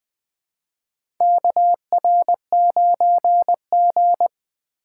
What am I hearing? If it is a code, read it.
Morse code, heard as a keyed tone, about 20 wpm, 700 Hz. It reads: KR9G
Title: USA Call Signs - 4 Characters - 8x Character Spacing - 20wpm